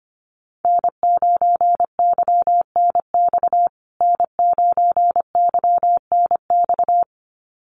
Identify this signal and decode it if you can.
Morse code, heard as a keyed tone, about 25 wpm, 700 Hz. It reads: N9YNX N9YNX
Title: USA Call Signs Repeated 2x 25wpm